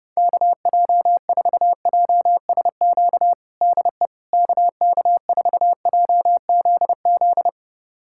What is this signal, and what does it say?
Morse code, heard as a keyed tone, about 30 wpm, 700 Hz. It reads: KJ4JSQ DE KK4JZZ
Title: USA Callsign DE Callsign 30wpm